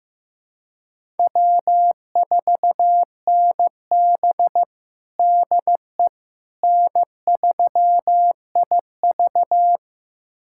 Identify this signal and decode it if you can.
Morse code, heard as a keyed tone, about 16 wpm, 700 Hz. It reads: W4NB DE N3IV
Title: USA Callsign DE Callsign - 4 Characters - 15wpm